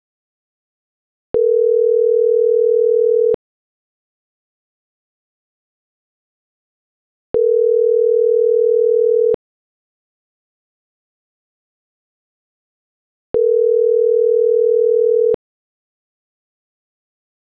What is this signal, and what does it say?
Ringback tone (North America), 440 + 480 Hz.